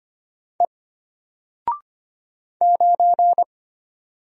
Morse code, heard as a keyed tone, about 25 wpm, 700 Hz. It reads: E 9
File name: Single-Letter-Number-25wpm